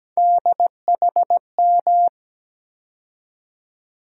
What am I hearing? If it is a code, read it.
Morse code, heard as a keyed tone, about 17 wpm, 700 Hz. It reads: DHM
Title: USA Call Sign Postfixes - 3 Letters 17wpm